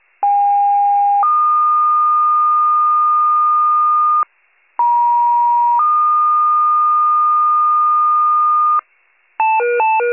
Two-tone page, heard as a one-second tone, then a three-second tone, about 810 then 1220 Hz, about 950 then 1220 Hz. It is heard from 0.2 to 8.8 s.